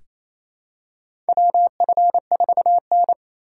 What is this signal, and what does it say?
Morse code, heard as a keyed tone, about 28 wpm, 700 Hz. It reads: WF4N
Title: USA Call Signs - 4 Characters - 2x Character Spacing - 28wpm